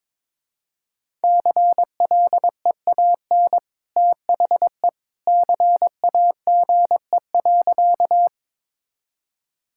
Morse code, heard as a keyed tone, about 22 wpm, 700 Hz. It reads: CLEAN THE CAGE.
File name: practice / 22wpm / Easy Sentences with Hints - 22wpm